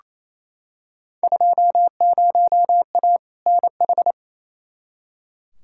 Morse code, heard as a keyed tone, about 27 wpm, 700 Hz. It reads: 20A NH
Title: Field Day Exchanges Encore - 28wpm